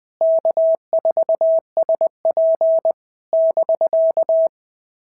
Morse code, heard as a keyed tone, about 20 wpm, 650 Hz. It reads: K4SP <BK>
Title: POTA QSO Practice - 20wpm